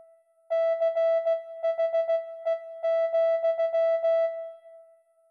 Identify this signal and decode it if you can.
Morse code, heard as a keyed tone, about 16 wpm, 660 Hz. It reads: CHE,